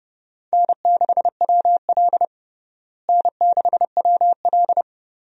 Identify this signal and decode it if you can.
Morse code, heard as a keyed tone, about 30 wpm, 700 Hz. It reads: N6WL N6WL
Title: USA Call Signs - 4 Characters - Repeated 3x - 30wpm